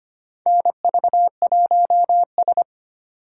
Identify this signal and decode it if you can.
Morse code, heard as a keyed tone, about 25 wpm, 700 Hz. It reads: NV1S
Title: USA Call Signs - 4 Characters - 4x Character Spacing - 25wpm